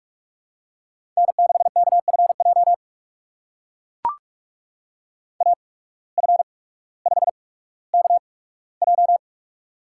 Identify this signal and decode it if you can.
Morse code, heard as a keyed tone, about 44 wpm, spaced as for 19 wpm, 690 Hz. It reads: N6KFJ AF5KJ
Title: USA Call Signs - 5 Characters - 8x Character Spacing - 45wpm